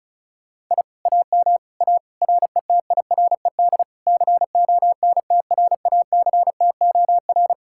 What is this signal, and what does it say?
Morse code, heard as a keyed tone, about 44 wpm, 700 Hz. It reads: I AM A RETIRED CONTRACTOR